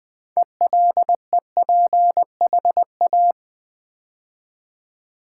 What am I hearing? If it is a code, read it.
Morse code, heard as a keyed tone, about 20 wpm, 700 Hz. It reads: ELEPHA
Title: Build A Word - 3 Letters At A Time - 8-Letter Words - 20wpm